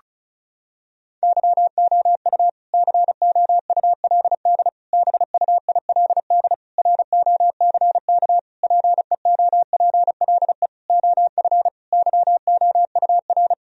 Morse code, heard as a keyed tone, about 34 wpm, 700 Hz. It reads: YOU COULD BUILD ROCK PEOPLE OF YOUR
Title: Sentences from Top 1000 Words - Review Part 2 - 35wpm